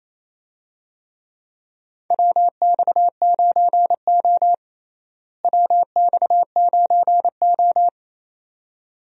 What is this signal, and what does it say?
Morse code, heard as a keyed tone, about 28 wpm, 700 Hz. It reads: WX9O WX9O
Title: USA Call Signs - 4 Characters - Repeated 2x - 28wpm